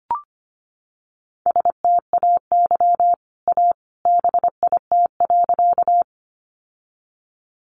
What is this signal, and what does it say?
Morse code, heard as a keyed tone, about 25 wpm, 700 Hz. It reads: STAY A BIT.
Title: Sentences from Top 900 Words 25wpm